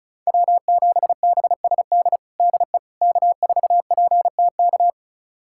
Morse code, heard as a keyed tone, about 34 wpm, 700 Hz. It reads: W7BSD DE K4PTK
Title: USA Callsign DE Callsign - 5 Characters - 35wpm